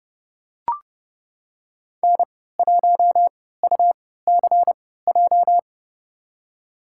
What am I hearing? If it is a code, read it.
Morse code, heard as a keyed tone, about 30 wpm, spaced as for 17 wpm, 700 Hz. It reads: N1UCJ
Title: USA Call Signs - 5 Characters - 3x Character Spacing - 30wpm